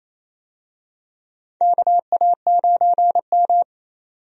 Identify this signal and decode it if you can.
Morse code, heard as a keyed tone, about 28 wpm, 700 Hz. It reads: KA9M
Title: USA Call Signs - 4 Characters - 28wpm